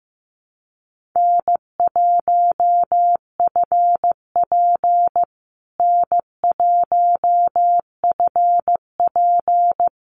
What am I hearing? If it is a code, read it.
Morse code, heard as a keyed tone, about 15 wpm, 700 Hz. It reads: N1FP N1FP
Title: USA Call Signs Repeated 3x 15wpm